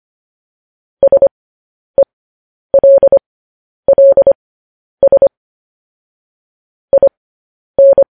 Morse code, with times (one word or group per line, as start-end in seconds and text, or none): 1.0-5.3 SELLS
6.9-8.0 IN